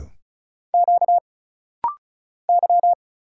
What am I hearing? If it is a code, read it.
Morse code, heard as a keyed tone, about 36 wpm, 700 Hz. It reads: Q Y
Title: Single Letters - Rapid-Fire 35wpm